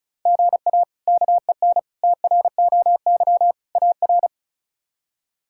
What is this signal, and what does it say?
Morse code, heard as a keyed tone, about 36 wpm, 700 Hz. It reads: GA KEN TROY AR
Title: K1USN SST QSO Practice - 35wpm